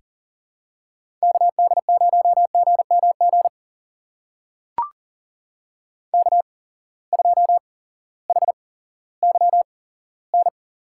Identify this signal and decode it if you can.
Morse code, heard as a keyed tone, about 40 wpm, spaced as for 18 wpm, 700 Hz. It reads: KD0GMG K2HYN
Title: USA Call Signs - 8x Character Spacing - 40wpm